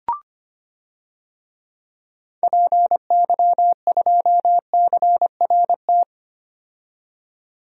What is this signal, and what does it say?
Morse code, heard as a keyed tone, about 25 wpm, 700 Hz. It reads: PY2CRT
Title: DXCC-100 Call Signs - 6 Characters - 25wpm